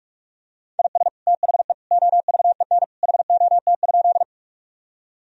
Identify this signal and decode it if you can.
Morse code, heard as a keyed tone, about 46 wpm, 700 Hz. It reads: IS THE OVEN HOT?
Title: Sentences from Top 1000 Words - Review Part 4 - 45wpm